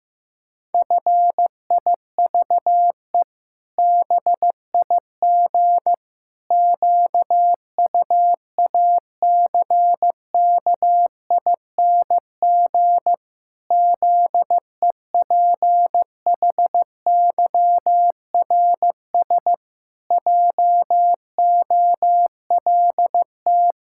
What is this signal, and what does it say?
Morse code, heard as a keyed tone, about 15 wpm, 700 Hz. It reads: FIVE BIG QUACKING ZEPHYRS JOLT
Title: Pangram Challenge v2 - 15wpm